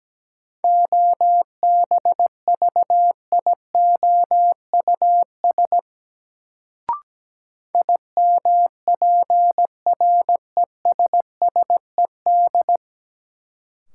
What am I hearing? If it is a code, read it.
Morse code, heard as a keyed tone, about 17 wpm, 700 Hz. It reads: OBVIOUS IMPRESSED
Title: Sentences from Top 1600 Words - New Word Drill Alt - 17wpm